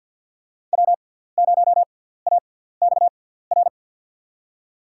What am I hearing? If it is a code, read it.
Morse code, heard as a keyed tone, about 52 wpm, spaced as for 17 wpm, 700 Hz. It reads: W0AXR
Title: USA Call Signs - 5 Characters - 6x Character Spacing - 50wpm